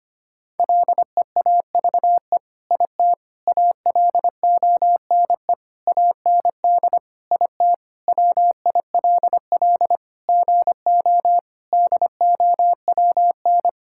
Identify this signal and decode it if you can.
Morse code, heard as a keyed tone, about 25 wpm, 700 Hz. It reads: LEAVE IT ALONE AND IT WILL GO DOWN
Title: Sentences from Top 1000 Words - Review Part 5 - 25wpm